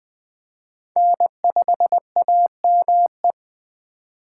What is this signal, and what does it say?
Morse code, heard as a keyed tone, about 20 wpm, 700 Hz. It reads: N5AME